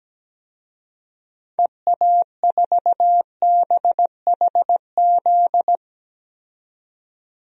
Morse code, heard as a keyed tone, about 17 wpm, 700 Hz. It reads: EA4BHZ
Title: DXCC-100 Call Signs - 6 Characters - 17wpm